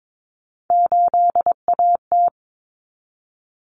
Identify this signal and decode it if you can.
Morse code, heard as a keyed tone, about 22 wpm, 700 Hz. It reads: 8AT
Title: Common Cut Numbers - 3-Digit 22wpm